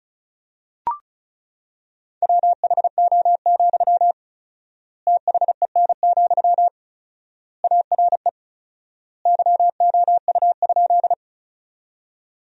Morse code, heard as a keyed tone, about 36 wpm, 700 Hz. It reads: WHO, THEN, ARE YOU?
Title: 4x Word-Spacing - Sentences from Top 100 Words - 35wpm